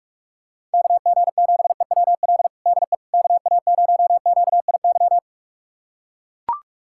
Morse code, heard as a keyed tone, about 41 wpm, 690 Hz. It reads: KG7EWL DE KA0QIY
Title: USA Callsign DE Callsign - 6 Characters - 45wpm